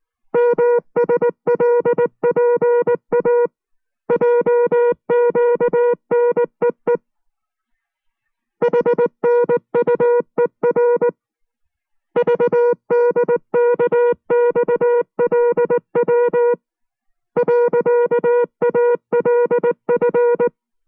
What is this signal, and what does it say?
Morse code, heard as a keyed tone, about 19 wpm, 480 Hz. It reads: MSLPA JQNEE HNUER VDKXLW .ALF